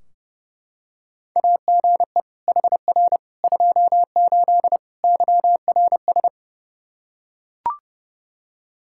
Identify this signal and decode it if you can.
Morse code, heard as a keyed tone, about 32 wpm, 700 Hz. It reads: AGE HR 28 YRS
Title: Practice QSO - Exchange 3 - More Info - 30wpm